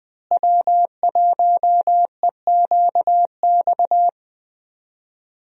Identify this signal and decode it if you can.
Morse code, heard as a keyed tone, about 20 wpm, 700 Hz. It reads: W1EQX
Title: USA Call Signs - 5 Characters - Repeated 3x - 20wpm